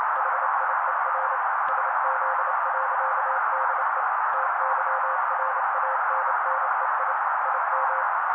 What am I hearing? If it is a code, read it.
Morse code, heard as a keyed tone, about 28 wpm, 530 Hz. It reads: HIER SZ.DE TYRANNEI IM
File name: daily / 27wpm noise